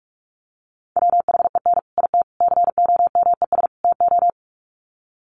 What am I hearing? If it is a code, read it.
Morse code, heard as a keyed tone, about 44 wpm, 700 Hz. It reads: WHEN IT COMES TO
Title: Sets of 4 Words 45wpm